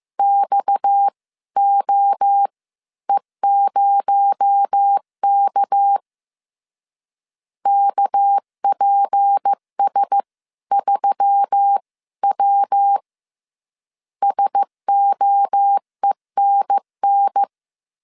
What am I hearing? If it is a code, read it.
Morse code, heard as a keyed tone, about 15 wpm, spaced as for 13 wpm, 800 Hz. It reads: XOE0K KPS3W SOENN